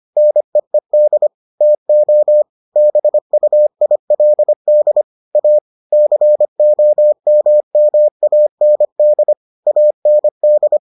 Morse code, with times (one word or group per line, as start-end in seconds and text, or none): 0.2-1.3 NEED
1.6-2.4 TO
2.8-5.0 BUILD
5.3-5.6 A
5.9-9.3 COMMAND
9.7-10.8 AND